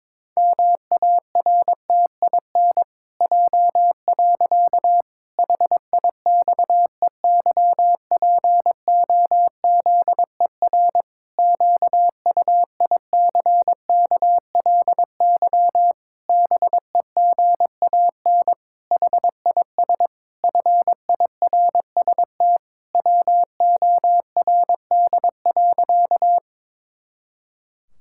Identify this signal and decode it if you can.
Morse code, heard as a keyed tone, about 22 wpm, 700 Hz. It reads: MARTIN J. HIXEYPOZER QUICKLY BEGAN HIS FIRST WORD.
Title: Pangram Challenge v2 - 22wpm